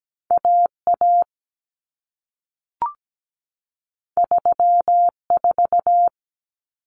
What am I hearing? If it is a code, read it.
Morse code, heard as a keyed tone, about 17 wpm, 700 Hz. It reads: AA 34
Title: Common Cut Numbers - 2-Digit 17wpm